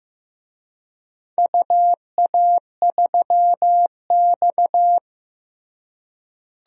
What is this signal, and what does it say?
Morse code, heard as a keyed tone, about 15 wpm, 700 Hz. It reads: UA3X
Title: DXCC-100 Call Signs - 4 Characters - Repeated 2x - 15wpm